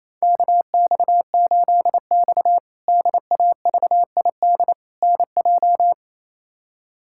Morse code, heard as a keyed tone, about 28 wpm, 700 Hz. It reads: KX8X DAVID NJ
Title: K1USN SST QSO Practice - 28wpm